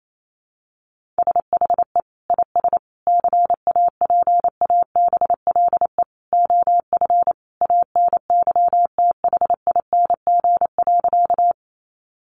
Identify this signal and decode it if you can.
Morse code, heard as a keyed tone, about 28 wpm, 700 Hz. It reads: SHE IS CAPABLE OF ANYTHING.
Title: Sentences from Top 1200 Words 28wpm